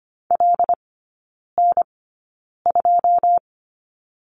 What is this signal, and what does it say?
Morse code, heard as a keyed tone, about 25 wpm, spaced as for 9 wpm, 700 Hz. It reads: LN2
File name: Lesson 14e - Number 2 - Three Characters 25wpm